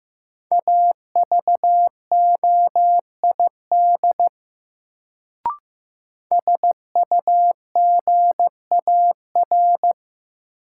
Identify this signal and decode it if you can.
Morse code, heard as a keyed tone, about 15 wpm, 700 Hz. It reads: AVOID SUGAR
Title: Sentences from Top 1200 Words - New Word Drill Alt - 15wpm